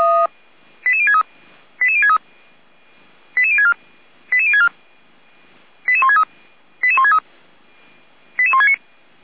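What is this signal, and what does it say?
Five-tone selective call (ZVEI), heard as quick runs of five tones, tones of about 70 ms. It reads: 88063 88063 88064 88064 88163 88163 88169
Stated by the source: ZVEI five-tone call received on SDR.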